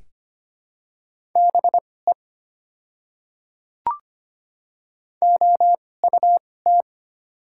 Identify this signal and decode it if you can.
Morse code, heard as a keyed tone, about 25 wpm, spaced as for 18 wpm, 700 Hz. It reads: BE OUT
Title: 2x Character Spacing - Top 100 Words - 25wpm